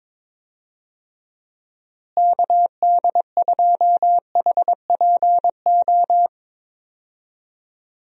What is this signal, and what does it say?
Morse code, heard as a keyed tone, about 22 wpm, 700 Hz. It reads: KD2HPO
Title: USA Call Signs - 6 Characters - Repeated 2x - 22wpm